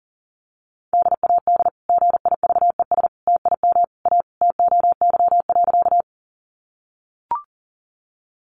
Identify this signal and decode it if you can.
Morse code, heard as a keyed tone, about 40 wpm, 700 Hz. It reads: DAD GIVES TIM A TOY.
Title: Easy Phrases 40wpm